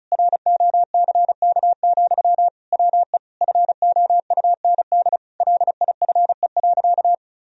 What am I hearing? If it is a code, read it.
Morse code, heard as a keyed tone, about 36 wpm, 700 Hz. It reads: ROCK, WE FOUND LIFE.
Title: Sentences from Top 1000 Words - Review Part 2 - 35wpm